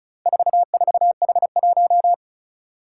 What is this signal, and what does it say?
Morse code, heard as a keyed tone, about 34 wpm, 700 Hz. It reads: 44H1
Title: Four Challenging Letter-Number Combinations 35wpm